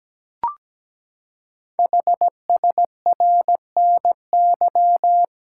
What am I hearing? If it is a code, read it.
Morse code, heard as a keyed tone, about 17 wpm, 700 Hz. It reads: HSRNY
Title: Code Groups - 5 Letters - 17wpm